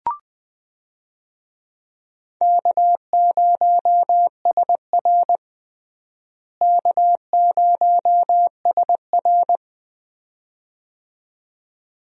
Morse code, heard as a keyed tone, about 20 wpm, 700 Hz. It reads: K0SR K0SR
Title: USA Call Signs - 4 Characters - Repeated 2x - 20wpm